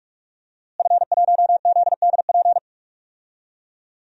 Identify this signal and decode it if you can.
Morse code, heard as a keyed tone, about 46 wpm, 700 Hz. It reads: F1ZDP
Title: DXCC-100 Call Signs - 5 Characters - Repeated 3x - 45wpm